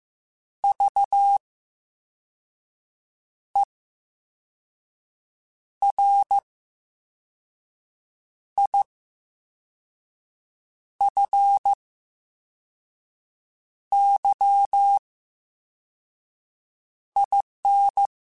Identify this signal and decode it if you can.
Morse code, heard as a keyed tone, about 15 wpm, spaced as for 4 wpm, 800 Hz. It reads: VERIFYIN
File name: Special INGs / Special INGs-mp3s / Slow ing @13